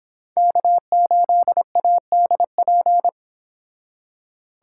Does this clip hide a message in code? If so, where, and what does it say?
Yes: Morse code from 0.4 to 3.1 s, heard as a keyed tone, about 29 wpm, 700 Hz. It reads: K8ADP